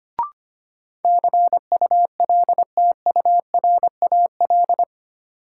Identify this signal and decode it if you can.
Morse code, heard as a keyed tone, about 25 wpm, 700 Hz. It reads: CULTURAL
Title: Sentences from Top 1500 Words - New Word Drill - 25wpm